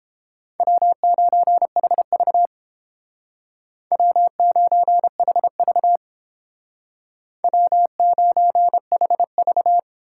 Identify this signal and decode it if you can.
Morse code, heard as a keyed tone, about 28 wpm, 700 Hz. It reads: W9HV W9HV W9HV